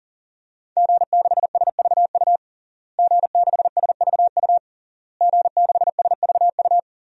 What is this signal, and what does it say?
Morse code, heard as a keyed tone, about 40 wpm, 700 Hz. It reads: G6SVU G6SVU G6SVU
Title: DXCC-100 Call Signs - 5 Characters - Repeated 3x - 40wpm